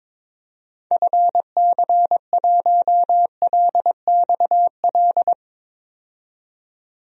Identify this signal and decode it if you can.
Morse code, heard as a keyed tone, about 22 wpm, 700 Hz. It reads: FC1LXL
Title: DXCC-100 Call Signs - 6 Characters - 22wpm